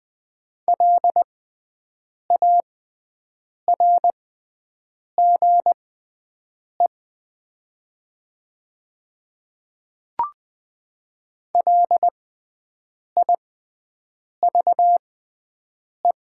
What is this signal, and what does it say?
Morse code, heard as a keyed tone, about 20 wpm, spaced as for 7 wpm, 700 Hz. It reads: LARGE LIVE
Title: 6x Character Spacing - Top 200 Words - 20wpm